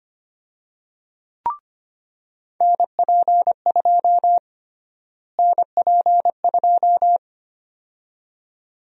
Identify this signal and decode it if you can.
Morse code, heard as a keyed tone, about 25 wpm, 700 Hz. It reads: NP2 NP2
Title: USA Call Sign Prefixes - Repeated 2x - 25wpm